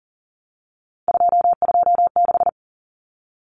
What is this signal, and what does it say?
Morse code, heard as a keyed tone, about 40 wpm, 700 Hz. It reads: 226
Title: Numbers - 3-Digit 40wpm